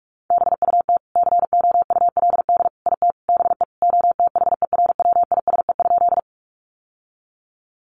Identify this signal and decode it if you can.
Morse code, heard as a keyed tone, about 44 wpm, 700 Hz. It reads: BUT COULD IT BE OTHERWISE?
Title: Sentences from Top 1200 Words 45wpm